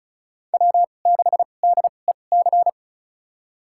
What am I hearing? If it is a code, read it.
Morse code, heard as a keyed tone, about 36 wpm, spaced as for 25 wpm, 700 Hz. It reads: W6DEC
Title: USA Call Signs - 5 Characters - 2x Character Spacing - 35wpm